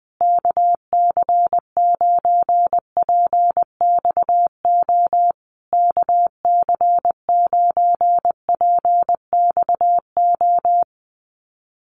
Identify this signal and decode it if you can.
Morse code, heard as a keyed tone, about 20 wpm, 700 Hz. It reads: KC9PXO KC9PXO
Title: USA Call Signs Repeated 3x 20wpm